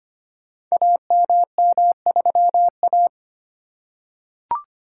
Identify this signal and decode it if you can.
Morse code, heard as a keyed tone, about 25 wpm, 700 Hz. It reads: AMM3A
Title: Code Groups - 5 Letter:Numbers - 25wpm